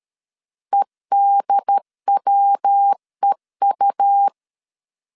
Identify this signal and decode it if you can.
Morse code, heard as a keyed tone, about 13 wpm, 800 Hz. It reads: EDWEU
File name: CW / DET / 1 3 1500C S B 12wpm 2828